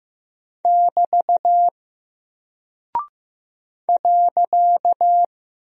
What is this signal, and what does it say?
Morse code, heard as a keyed tone, about 15 wpm, 700 Hz. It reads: <BT> .
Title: Single Character (Letter:Number:Punctuation:Prosign) Alt 15wpm